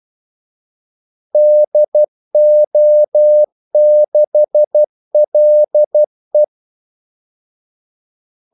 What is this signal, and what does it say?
Morse code, heard as a keyed tone, about 12 wpm, 600 Hz. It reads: DO6LE
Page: 2 ekstra ordmellemrum mellem kaldesignalerne